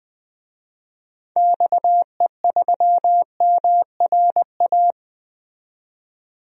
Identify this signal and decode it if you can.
Morse code, heard as a keyed tone, about 20 wpm, 700 Hz. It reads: XE3MRA
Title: DXCC-100 Call Signs - 6 Characters - 20wpm